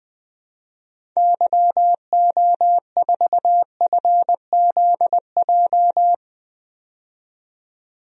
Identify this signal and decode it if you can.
Morse code, heard as a keyed tone, about 20 wpm, 700 Hz. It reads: YO4FZJ
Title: DXCC-100 Call Signs - 6 Characters - 20wpm